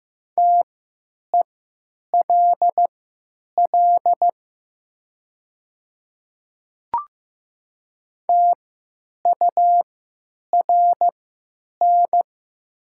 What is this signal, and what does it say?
Morse code, heard as a keyed tone, about 15 wpm, spaced as for 9 wpm, 700 Hz. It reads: TELL TURN
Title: 3x Character Spacing - Top 200 Words - 15wpm